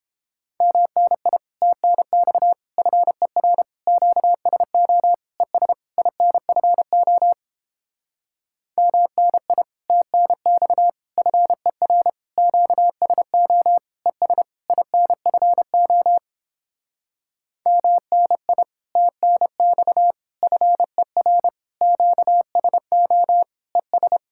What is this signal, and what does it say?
Morse code, heard as a keyed tone, about 29 wpm, 700 Hz. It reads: MNI TNX FER QSO ES INFO MNI TNX FER QSO ES INFO MNI TNX FER QSO ES